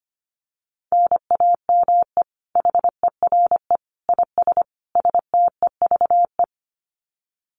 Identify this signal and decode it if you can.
Morse code, heard as a keyed tone, about 25 wpm, 700 Hz. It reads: NAME HERE IS STEVE
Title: Common QSO Phrases 25wpm